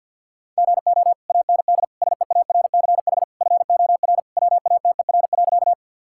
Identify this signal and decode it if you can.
Morse code, heard as a keyed tone, about 50 wpm, 700 Hz. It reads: GO AND SEARCH FOR WATER.
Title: Sentences from Top 1000 Words - Review Part 5 - 50wpm